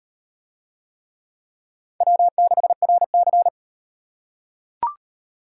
Morse code, heard as a keyed tone, about 38 wpm, 700 Hz. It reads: W6RC